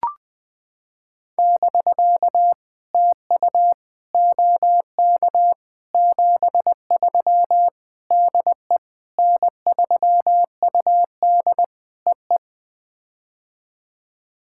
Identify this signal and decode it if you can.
Morse code, heard as a keyed tone, about 20 wpm, 700 Hz. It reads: <BK> TU OK 73 DE N3UD EE
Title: POTA QSO Practice - 20wpm